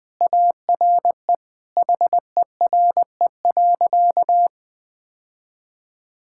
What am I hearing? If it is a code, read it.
Morse code, heard as a keyed tone, about 21 wpm, 700 Hz. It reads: ARE HERE.
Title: Sentences with 5-Word Phrases - 20wpm